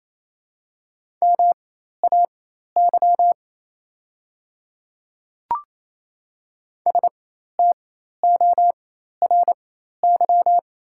Morse code, heard as a keyed tone, about 28 wpm, spaced as for 13 wpm, 700 Hz. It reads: MAY STORY
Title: 4x Character Spacing - Top 300 Words - 28wpm